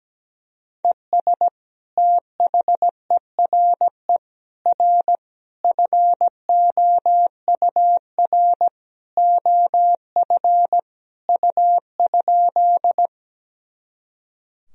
Morse code, heard as a keyed tone, about 17 wpm, 700 Hz. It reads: ES THERE R FOUR OF U?
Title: Sentences from Top 300 Words and Abbreviations 17wpm